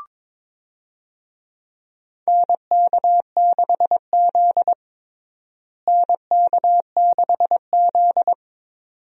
Morse code, heard as a keyed tone, about 22 wpm, 700 Hz. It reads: NK6Z NK6Z
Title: USA Call Signs - 4 Characters - Repeated 3x - 22wpm